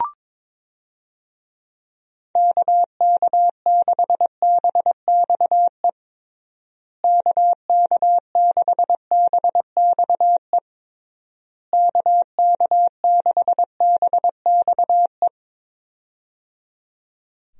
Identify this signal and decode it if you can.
Morse code, heard as a keyed tone, about 22 wpm, 700 Hz. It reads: KK6BXE KK6BXE KK6BXE
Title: USA Call Signs - 6 Characters - Repeated 3x - 22wpm